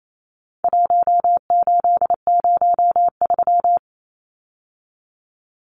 Morse code, heard as a keyed tone, about 28 wpm, 700 Hz. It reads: 1803
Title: Numbers - 4-Digit 28wpm